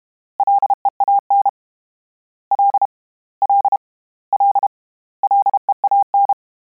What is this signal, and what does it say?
Morse code, heard as a keyed tone, about 32 wpm, 800 Hz. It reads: LEAN L L L LEAN
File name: Combinations / Combinations-mp3s / Combinations@30